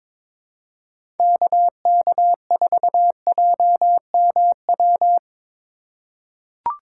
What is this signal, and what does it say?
Morse code, heard as a keyed tone, about 23 wpm, 700 Hz. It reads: KK4JMW
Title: USA Call Signs - 6 Characters - 4x Character Spacing - 22wpm